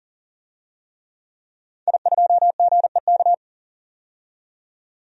Morse code, heard as a keyed tone, about 40 wpm, 700 Hz. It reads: I2GEK
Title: DXCC-100 Call Signs - 5 Characters - 40wpm